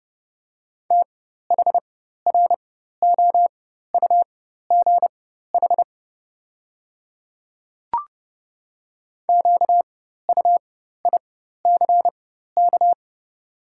Morse code, heard as a keyed tone, about 32 wpm, spaced as for 14 wpm, 700 Hz. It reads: THROUGH QUICK
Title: 4x Character Spacing - Top 500 Words - 30wpm